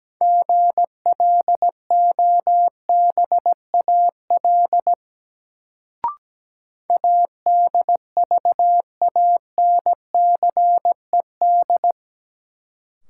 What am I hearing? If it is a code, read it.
Morse code, heard as a keyed tone, about 17 wpm, 700 Hz. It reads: GLOBAL ADVANCED
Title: Sentences from Top 1200 Words - New Word Drill - 17wpm